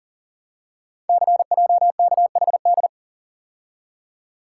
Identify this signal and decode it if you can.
Morse code, heard as a keyed tone, about 39 wpm, 700 Hz. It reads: CJKHD
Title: Code Groups - 5 Letter:Numbers - 40wpm